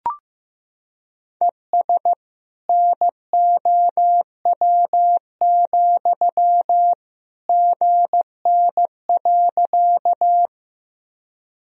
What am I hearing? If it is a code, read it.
Morse code, heard as a keyed tone, about 15 wpm, 700 Hz. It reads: ES NOW, GN.
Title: Sentences from Top 300 Words and Abbreviations 15wpm